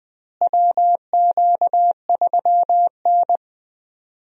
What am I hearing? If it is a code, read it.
Morse code, heard as a keyed tone, about 19 wpm, 700 Hz. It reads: WQ3N
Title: USA Call Signs - 4 Characters - Repeated 3x - 20wpm